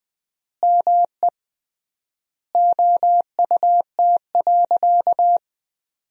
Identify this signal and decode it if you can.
Morse code, heard as a keyed tone, about 20 wpm, 700 Hz. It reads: ME OUT.
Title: 3x Word-Spacing - Sentences from Top 500 Words - 20wpm